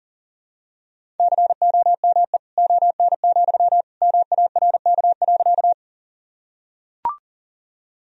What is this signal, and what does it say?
Morse code, heard as a keyed tone, about 37 wpm, 700 Hz. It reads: COME ON, MARK.
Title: Sentences from Top 1000 Words - Review Part 1 - 40wpm